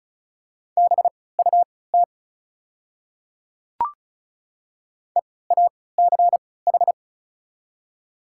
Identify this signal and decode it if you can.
Morse code, heard as a keyed tone, about 36 wpm, spaced as for 20 wpm, 700 Hz. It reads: BUT EACH
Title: 3x Character Spacing - Top 200 Words - 35wpm